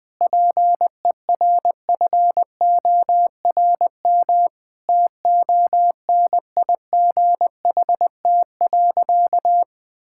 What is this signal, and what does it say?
Morse code, heard as a keyed tone, about 20 wpm, 700 Hz. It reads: PERFORM TONIGHT.
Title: Sentences from Top 1000 Words - Review Part 5 - 20wpm